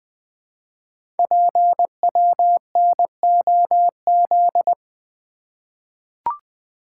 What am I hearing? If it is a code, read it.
Morse code, heard as a keyed tone, about 21 wpm, 700 Hz. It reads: PWNOZ
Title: Code Groups - 5 Letters - 20wpm